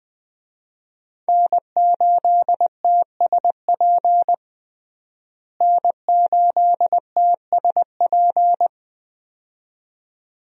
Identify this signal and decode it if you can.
Morse code, heard as a keyed tone, about 20 wpm, 700 Hz. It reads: N8TSP N8TSP
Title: USA Call Signs - 5 Characters - Repeated 2x - 20wpm